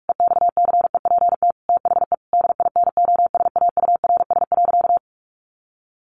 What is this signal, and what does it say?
Morse code, heard as a keyed tone, about 44 wpm, 700 Hz. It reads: EXCEPT THE DINOSAURS.
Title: A Sound Of Thunder 45wpm